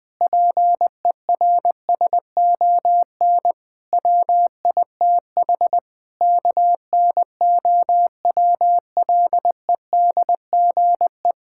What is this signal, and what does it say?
Morse code, heard as a keyed tone, about 20 wpm, 700 Hz. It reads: PERSON WITH KNOWLEDGE
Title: Sentences with 9-Word Phrases - Spoken Beginning - 20wpm